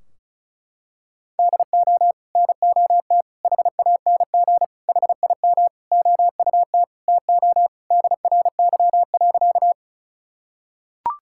Morse code, heard as a keyed tone, about 34 wpm, 700 Hz. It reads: DO NOT HANG HIM OUT TO DRY.
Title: Sentences from Top 1600 Words 35wpm